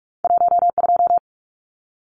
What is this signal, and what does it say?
Morse code, heard as a keyed tone, about 46 wpm, 700 Hz. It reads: 12
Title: Numbers - 2-Digit 45wpm